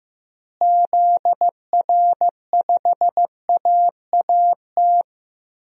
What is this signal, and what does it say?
Morse code, heard as a keyed tone, about 16 wpm, 700 Hz. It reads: ZR5AAT
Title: DXCC-100 Call Signs - 6 Characters - Repeated 2x - 15wpm